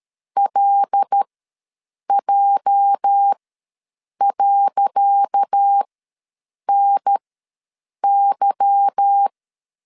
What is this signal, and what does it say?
Morse code, heard as a keyed tone, about 13 wpm, spaced as for 7 wpm, 800 Hz. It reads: LJ.NY
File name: LEZIONI / 10 SIGNS / 10C DOT ws 12wpm 3548